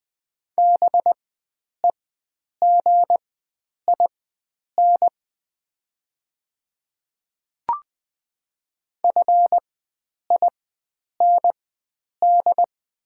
Morse code, heard as a keyed tone, about 20 wpm, spaced as for 9 wpm, 700 Hz. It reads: BEGIN FIND
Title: 4x Character Spacing - Top 300 Words - 20wpm